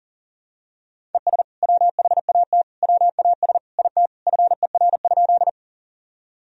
Morse code, heard as a keyed tone, about 37 wpm, 700 Hz. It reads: ES WHAT WAS IT FER?
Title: Sentences from Top 100 Words and Abbreviations 40wpm